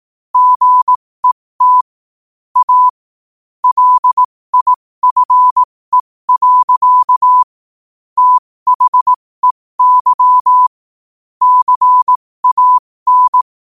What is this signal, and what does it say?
Morse code, heard as a keyed tone, about 18 wpm, spaced as for 15 wpm, 1000 Hz. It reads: GET A LIFE. THEY CAN
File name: qotdfiles / 15wpm